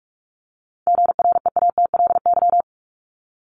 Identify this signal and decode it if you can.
Morse code, heard as a keyed tone, about 44 wpm, 700 Hz. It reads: GREATLY
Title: LY Suffix - Top 100 Words with Base Words 45wpm